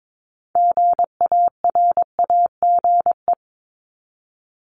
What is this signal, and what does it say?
Morse code, heard as a keyed tone, about 22 wpm, 700 Hz. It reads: GARAGE
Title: Top 1501-1750 Words 22wpm